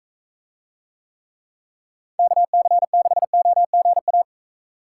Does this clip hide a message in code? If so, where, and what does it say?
Yes: Morse code from 2.2 to 4.2 s, heard as a keyed tone, about 41 wpm, 700 Hz. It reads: KC6OGA